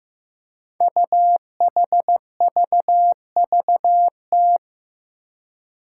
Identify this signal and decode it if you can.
Morse code, heard as a keyed tone, about 15 wpm, 700 Hz. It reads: UHVVT
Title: Five Challenging Letter-Number Combinations 15wpm